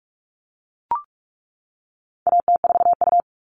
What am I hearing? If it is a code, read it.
Morse code, heard as a keyed tone, about 44 wpm, 700 Hz. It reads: AT4U
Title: All Cut Numbers - 4-Digit 45wpm